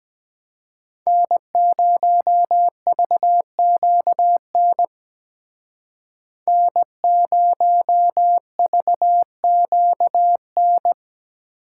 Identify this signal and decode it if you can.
Morse code, heard as a keyed tone, about 20 wpm, 700 Hz. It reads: N0VQN N0VQN
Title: Speed Racing - USA Call Signs 17wpm